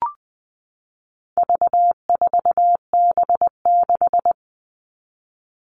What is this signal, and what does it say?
Morse code, heard as a keyed tone, about 20 wpm, 700 Hz. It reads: V4B6
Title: All Cut Numbers - 4-Digit 20wpm